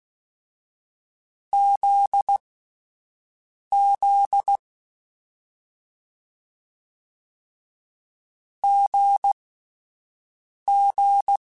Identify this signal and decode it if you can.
Morse code, heard as a keyed tone, about 16 wpm, spaced as for 5 wpm, 800 Hz. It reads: ZZ GG
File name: Difficult Combos / Difficult Combos-mp3s / GZ GZ7 @5